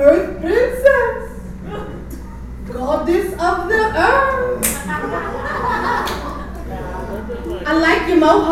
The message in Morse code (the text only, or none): none